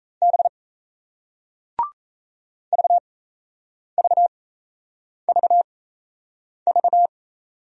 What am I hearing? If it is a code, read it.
Morse code, heard as a keyed tone, about 36 wpm, spaced as for 8 wpm, 700 Hz. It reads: B VVVV